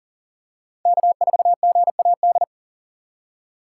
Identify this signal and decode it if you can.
Morse code, heard as a keyed tone, about 40 wpm, 700 Hz. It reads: K4GAD
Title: USA Call Signs - 2x Character Spacing - 40wpm